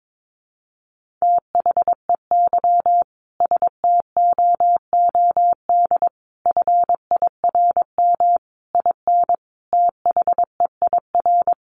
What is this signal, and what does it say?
Morse code, heard as a keyed tone, about 22 wpm, 700 Hz. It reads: THEY STOOD FIRM IN THEIR
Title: Sentences from Top 1400 Words 22wpm